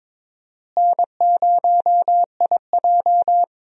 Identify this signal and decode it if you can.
Morse code, heard as a keyed tone, about 22 wpm, 700 Hz. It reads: N0IJ
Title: USA Call Signs - 4 Characters - Repeated 3x - 22wpm